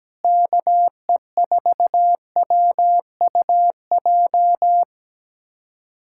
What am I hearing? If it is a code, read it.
Morse code, heard as a keyed tone, about 17 wpm, 700 Hz. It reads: KE4WUJ
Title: USA Callsign DE Callsign 17wpm